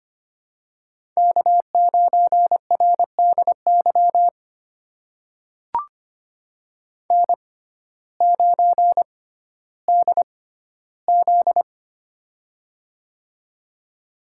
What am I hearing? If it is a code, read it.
Morse code, heard as a keyed tone, about 25 wpm, spaced as for 14 wpm, 700 Hz. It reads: K9RDY N9DZ
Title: USA Call Signs - 6x Character Spacing - 25wpm